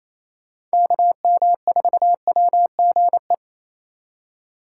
Morse code, heard as a keyed tone, about 37 wpm, 700 Hz. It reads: KM4WGE